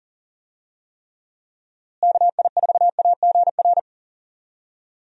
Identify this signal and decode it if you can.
Morse code, heard as a keyed tone, about 40 wpm, 700 Hz. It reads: KI4AGR
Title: USA Call Signs - 6 Characters - Repeated 2x - 40wpm